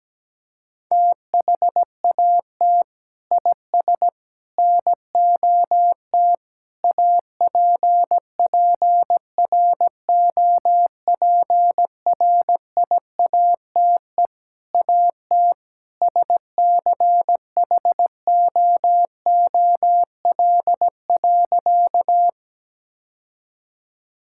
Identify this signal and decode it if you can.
Morse code, heard as a keyed tone, about 17 wpm, 700 Hz. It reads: THAT IS NOT APPROPRIATE AT SCHOOL.
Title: Sentences from Top 1000 Words - Review Part 5 - 17wpm